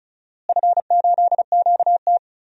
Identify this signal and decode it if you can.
Morse code, heard as a keyed tone, about 36 wpm, 700 Hz. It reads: F8QT
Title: DXCC-100 Call Signs - 4 Characters - Repeated 3x - 35wpm